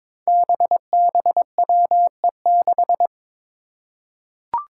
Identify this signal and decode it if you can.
Morse code, heard as a keyed tone, about 22 wpm, 700 Hz. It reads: BBWE6
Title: Five Challenging Letter-Number Combinations 22wpm